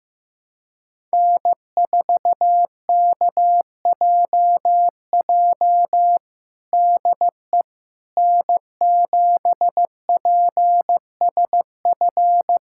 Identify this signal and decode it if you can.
Morse code, heard as a keyed tone, about 15 wpm, 700 Hz. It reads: N4KJJ DE N7PSF
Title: USA Callsign DE Callsign - 5 Characters - 15wpm